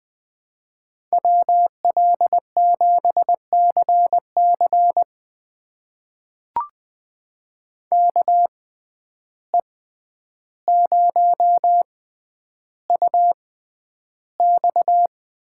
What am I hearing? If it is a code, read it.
Morse code, heard as a keyed tone, about 20 wpm, spaced as for 10 wpm, 700 Hz. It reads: WL7CC KE0UX
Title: USA Call Signs - 6x Character Spacing - 20wpm